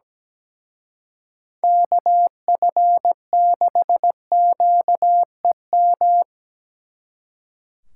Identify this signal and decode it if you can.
Morse code, heard as a keyed tone, about 19 wpm, 700 Hz. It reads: KF6QEM
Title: Speed Racing - USA 2x3 Call Signs 17wpm